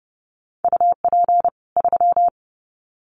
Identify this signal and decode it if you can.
Morse code, heard as a keyed tone, about 30 wpm, 700 Hz. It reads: UP 3
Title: Common QSO Phrases 30wpm